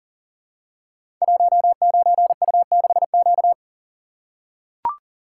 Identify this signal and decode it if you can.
Morse code, heard as a keyed tone, about 39 wpm, 700 Hz. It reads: 19U6Q
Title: Code Groups - 5 Letter:Numbers - 40wpm